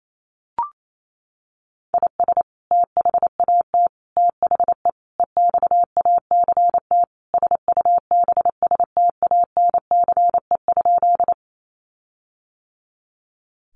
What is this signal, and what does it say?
Morse code, heard as a keyed tone, about 28 wpm, 700 Hz. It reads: IS THAT THE EXACT SUBSTANCE?
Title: Sentences from Top 1500 Words 28wpm